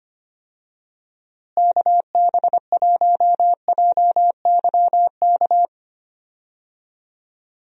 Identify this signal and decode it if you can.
Morse code, heard as a keyed tone, about 25 wpm, 700 Hz. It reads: KB1JYK
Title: USA Call Signs - 6 Characters - Repeated 2x - 25wpm